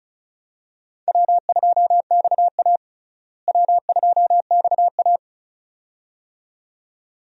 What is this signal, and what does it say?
Morse code, heard as a keyed tone, about 36 wpm, 700 Hz. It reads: W2XA W2XA
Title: USA Call Signs - 4 Characters - Repeated 2x - 35wpm